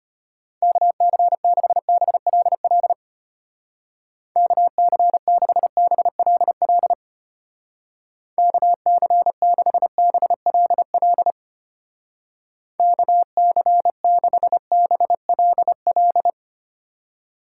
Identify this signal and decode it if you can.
Morse code, heard as a keyed tone, about 32 wpm, 700 Hz. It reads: KC6BLL KC6BLL KC6BLL KC6BLL